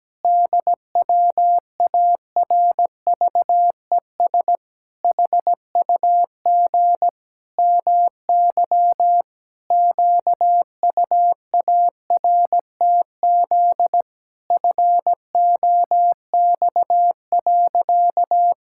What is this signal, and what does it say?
Morse code, heard as a keyed tone, about 17 wpm, 700 Hz. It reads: DWARVES HUG MY QUARTZ FOX.
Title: Pangram Challenge v2 - 17wpm